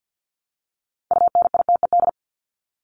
Morse code, heard as a keyed tone, about 50 wpm, 700 Hz. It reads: UNITED
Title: ED Suffix - Top 100 Words with Base Words 50wpm